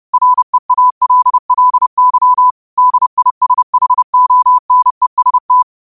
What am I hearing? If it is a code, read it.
Morse code, heard as a keyed tone, about 30 wpm, 1000 Hz. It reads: REALLY DISHONEST
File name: qotdfiles / 30wpm